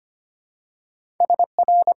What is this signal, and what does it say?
Morse code, heard as a keyed tone, about 25 wpm, 700 Hz. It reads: S R
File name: Top 90 QSO Elements - 6x Character Spacing - 25wpm